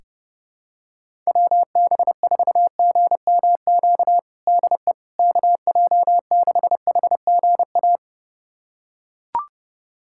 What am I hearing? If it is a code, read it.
Morse code, heard as a keyed tone, about 30 wpm, 700 Hz. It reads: WB4GMQ DE KJ6HGA
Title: USA Callsign DE Callsign - 6 Characters - 30wpm